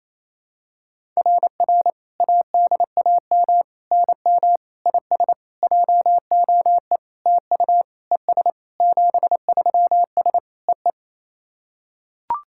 Morse code, heard as a keyed tone, about 28 wpm, 700 Hz. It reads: RR ADAM NM IS JOE TU ES 73S EE
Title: SOTA QSO Practice 28wpm